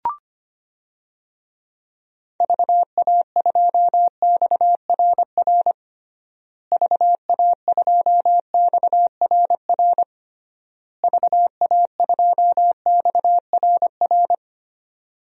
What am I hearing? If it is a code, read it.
Morse code, heard as a keyed tone, about 25 wpm, 700 Hz. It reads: VA2XRR VA2XRR VA2XRR
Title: DXCC-100 Call Signs - 6 Characters - Repeated 3x - 25wpm